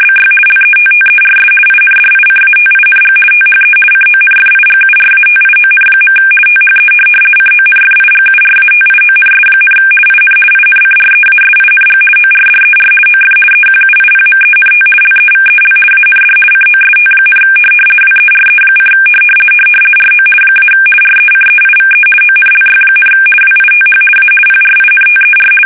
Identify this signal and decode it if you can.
RTTY, about 50 baud. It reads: BY LETTERS INDICATING THE CONFIGURATION. THE CONFIGURATIONS, IN INCREASING ORDER OF EQUIPMENT LEVEL AND COST, WERE: RO - RECEIVE ONLY, WITH A PRINTER
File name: fsk850-50B / teletype-corp